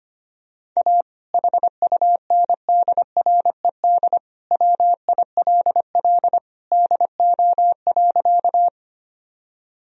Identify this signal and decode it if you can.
Morse code, heard as a keyed tone, about 25 wpm, 700 Hz. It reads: A HUNDRED WILL DO.
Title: Sentences from Top 1000 Words - Review Part 2 - 25wpm